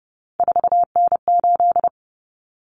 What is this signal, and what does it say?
Morse code, heard as a keyed tone, about 30 wpm, 700 Hz. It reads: 4N8
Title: Common Cut Numbers - 3-Digit 30wpm